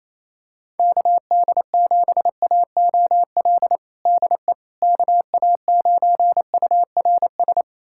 Morse code, heard as a keyed tone, about 28 wpm, 700 Hz. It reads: KD7AOL DE KA9URS
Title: USA Callsign DE Callsign 28wpm